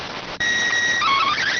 SSTV (slow-scan television), PD 290.